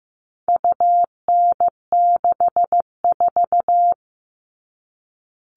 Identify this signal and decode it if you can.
Morse code, heard as a keyed tone, about 15 wpm, 700 Hz. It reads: UN64
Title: All Cut Numbers - 4-Digit 15wpm